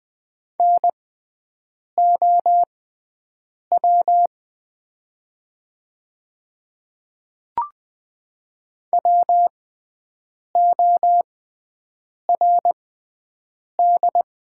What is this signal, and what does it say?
Morse code, heard as a keyed tone, about 20 wpm, spaced as for 7 wpm, 700 Hz. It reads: NOW WORD
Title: 6x Character Spacing - Top 100 Words - 20wpm